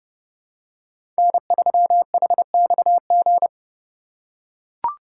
Morse code, heard as a keyed tone, about 30 wpm, 700 Hz. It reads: N3HXG
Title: USA Call Signs - 5 Characters - 3x Character Spacing - 30wpm